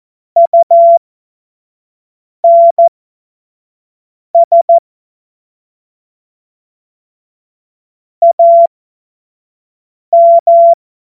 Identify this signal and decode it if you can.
Morse code, heard as a keyed tone, about 14 wpm, spaced as for 5 wpm, 680 Hz. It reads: UNS AM
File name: Witz-21-Oktober-2025-25-F70